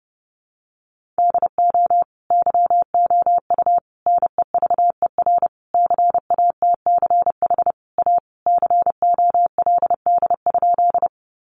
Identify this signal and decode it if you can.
Morse code, heard as a keyed tone, about 30 wpm, 700 Hz. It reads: DO YOU NEVER CATCH A COLD?
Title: Sentences from Top 1500 Words 30wpm